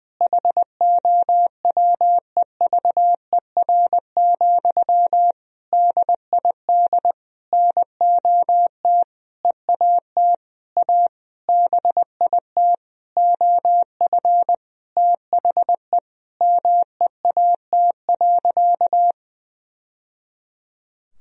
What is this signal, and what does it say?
Morse code, heard as a keyed tone, about 21 wpm, 700 Hz. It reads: HOWEVER, DID NOT EAT A BIT OF THE MEAT.
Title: Sentences from Top 1000 Words - Review Part 3 - 20wpm